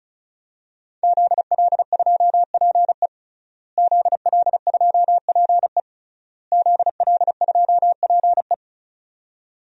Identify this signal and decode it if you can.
Morse code, heard as a keyed tone, about 34 wpm, 700 Hz. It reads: ZL2PE ZL2PE ZL2PE
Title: DXCC-100 Call Signs - 5 Characters - Repeated 3x - 35wpm